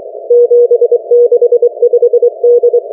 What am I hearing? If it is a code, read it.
Morse code, heard as a keyed tone, about 23 wpm, 480 Hz. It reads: 765D